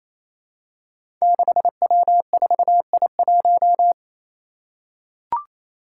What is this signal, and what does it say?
Morse code, heard as a keyed tone, about 28 wpm, 700 Hz. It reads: 6W4I1
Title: Five Challenging Letter-Number Combinations 28wpm